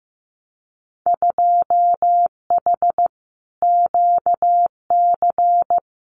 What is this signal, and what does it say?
Morse code, heard as a keyed tone, about 15 wpm, 700 Hz. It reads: 2H QC
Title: Winter Field Day Exchanges 15wpm